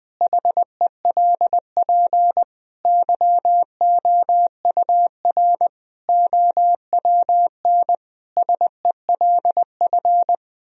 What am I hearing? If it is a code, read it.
Morse code, heard as a keyed tone, about 19 wpm, 700 Hz. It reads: HELP YOUR OWN SELF
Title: Sentences from Top 1000 Words - Review Part 1 - 20wpm